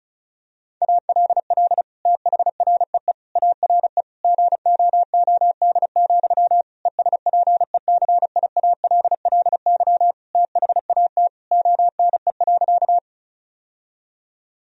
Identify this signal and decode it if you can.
Morse code, heard as a keyed tone, about 36 wpm, 700 Hz. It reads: ALL THREE ARE GOOD, ESPECIALLY THAT ONE.
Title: Sentences from Top 1000 Words - Review Part 4 - 35wpm